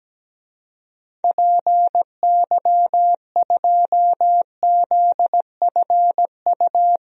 Morse code, heard as a keyed tone, about 17 wpm, 700 Hz. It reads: PY2ZFU
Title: DXCC-100 Call Signs - 6 Characters - 17wpm